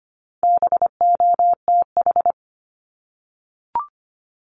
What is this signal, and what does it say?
Morse code, heard as a keyed tone, about 25 wpm, 700 Hz. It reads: BOTH
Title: Top 500 Words Repeated 25wpm